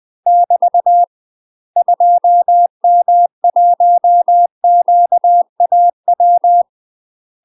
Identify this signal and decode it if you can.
Morse code, heard as a keyed tone, about 20 wpm, 700 Hz. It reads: <BT> 2M1QAW
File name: UK MASTER QSOs TRAINING / 20WPM / UK MASTER QSOs 0047